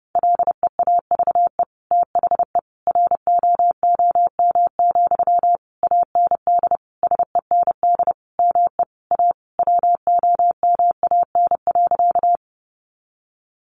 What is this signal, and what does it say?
Morse code, heard as a keyed tone, about 28 wpm, 700 Hz. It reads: LEAVE THE ROOM, AND SEND ME A WOMAN.
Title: Sentences from Top 1500 Words 30wpm